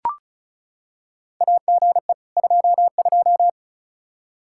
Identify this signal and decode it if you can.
Morse code, heard as a keyed tone, about 36 wpm, 700 Hz. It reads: AGE 22
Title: Practice QSO - Exchange 3 - More Info - 35wpm